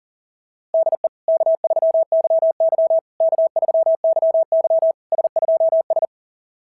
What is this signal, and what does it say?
Morse code, heard as a keyed tone, about 40 wpm, 660 Hz. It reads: DE K3YY K3YY S2S
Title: SOTA QSO Practice 40wpm